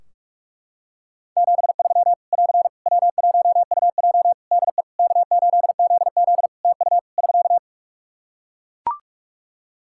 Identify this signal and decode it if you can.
Morse code, heard as a keyed tone, about 41 wpm, 700 Hz. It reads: 73 <AR> W1UJ DE K8ZZ TU <SK>
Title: Practice QSO - Exchange 4 - Ending - 45wpm